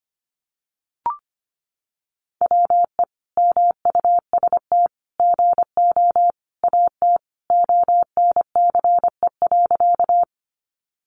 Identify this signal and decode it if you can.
Morse code, heard as a keyed tone, about 25 wpm, 700 Hz. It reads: WE MUST GO AT ONCE.
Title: Sentences from Top 300 Words 25wpm